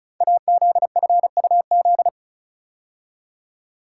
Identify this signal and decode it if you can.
Morse code, heard as a keyed tone, about 36 wpm, 700 Hz. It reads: AZFUZ
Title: Code Groups - 5 Letters - 35wpm